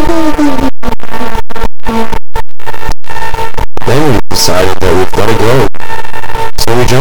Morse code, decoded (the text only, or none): none